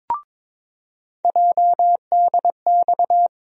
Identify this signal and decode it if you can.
Morse code, heard as a keyed tone, about 22 wpm, 700 Hz. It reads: JDX
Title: USA Call Sign Postfixes - 3 Letters 22wpm